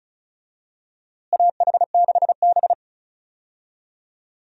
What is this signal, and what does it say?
Morse code, heard as a keyed tone, about 36 wpm, 700 Hz. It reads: AH6B
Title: USA Call Signs - 4 Characters - Repeated 2x - 35wpm